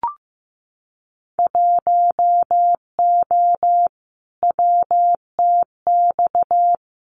Morse code, heard as a keyed tone, about 15 wpm, 700 Hz. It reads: 1O WTX
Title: Winter Field Day Exchanges Encore 15wpm